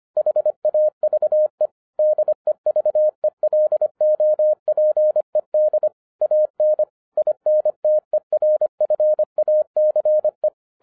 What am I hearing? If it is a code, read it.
Morse code, heard as a keyed tone, about 25 wpm, 610 Hz. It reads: HAVE DEVELOPED AN INTERFACE
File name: 2011-10-mp3 / n09-CW TO PS2 CONVERTER 25wpm